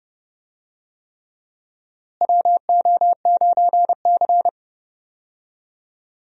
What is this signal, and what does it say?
Morse code, heard as a keyed tone, about 30 wpm, 700 Hz. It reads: WO9C
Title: USA Call Signs - 4 Characters - Repeated 3x - 30wpm